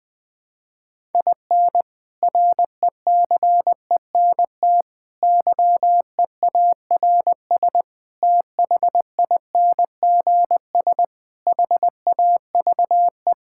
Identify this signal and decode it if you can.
Morse code, heard as a keyed tone, about 20 wpm, 700 Hz. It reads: IN RECENT YEARS THINGS HAVE
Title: Sentences with 3-Word Phrases - 20wpm